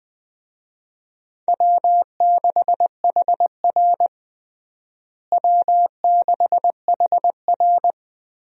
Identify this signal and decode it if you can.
Morse code, heard as a keyed tone, about 20 wpm, 690 Hz. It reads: W6HR W6HR
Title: USA Call Signs - 4 Characters - Repeated 3x - 20wpm